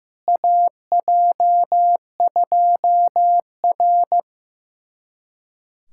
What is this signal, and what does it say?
Morse code, heard as a keyed tone, about 15 wpm, 700 Hz. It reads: AJ2R
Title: USA Callsign DE Callsign - 4 Characters - 15wpm